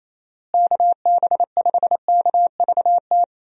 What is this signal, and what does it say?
Morse code, heard as a keyed tone, about 28 wpm, 700 Hz. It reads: KB5KVT
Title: USA Call Signs - 6 Characters - Repeated 3x - 28wpm